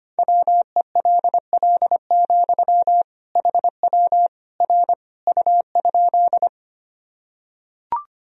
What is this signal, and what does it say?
Morse code, heard as a keyed tone, about 25 wpm, 700 Hz. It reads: WELL, HW R U?
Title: Sentences from Top 200 Words and Abbreviations 25wpm